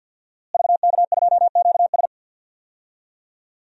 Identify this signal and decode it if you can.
Morse code, heard as a keyed tone, about 48 wpm, 700 Hz. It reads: VK2QS
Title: DXCC-100 Call Signs - 5 Characters - Repeated 2x - 50wpm